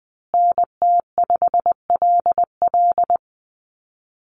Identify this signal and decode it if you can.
Morse code, heard as a keyed tone, about 20 wpm, 700 Hz. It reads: NT5LL
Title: USA Call Signs Repeated 3x 20wpm